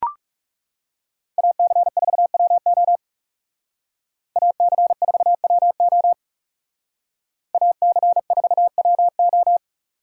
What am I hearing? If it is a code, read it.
Morse code, heard as a keyed tone, about 35 wpm, 700 Hz. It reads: AC4WO AC4WO AC4WO
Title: Speed Racing - USA 2x2 Call Signs 30wpm